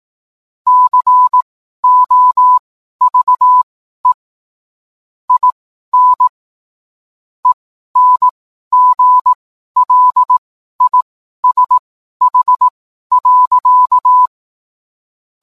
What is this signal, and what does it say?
Morse code, heard as a keyed tone, about 18 wpm, spaced as for 12 wpm, 1000 Hz. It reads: COVE IN ENGLISH.